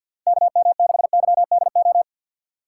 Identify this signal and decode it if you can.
Morse code, heard as a keyed tone, about 50 wpm, 700 Hz. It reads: KM6YDO
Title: USA Call Signs - 6 Characters - 6x Character Spacing - 50wpm